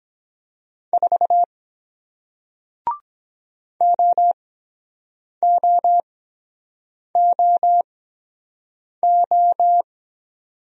Morse code, heard as a keyed tone, about 23 wpm, spaced as for 7 wpm, 700 Hz. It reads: 4 OOOO